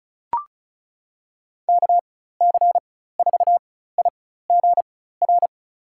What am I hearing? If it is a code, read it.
Morse code, heard as a keyed tone, about 34 wpm, spaced as for 16 wpm, 700 Hz. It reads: KC4IGR
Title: USA Call Signs - 6 Characters - 4x Character Spacing - 35wpm